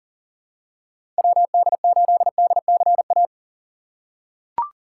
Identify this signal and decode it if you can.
Morse code, heard as a keyed tone, about 37 wpm, 700 Hz. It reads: WD8DCA
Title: USA Call Signs - 3x Character Spacing - 40wpm